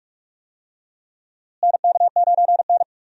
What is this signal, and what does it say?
Morse code, heard as a keyed tone, about 44 wpm, 700 Hz. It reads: NK9N
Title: USA Call Signs - 4 Characters - Repeated 3x - 45wpm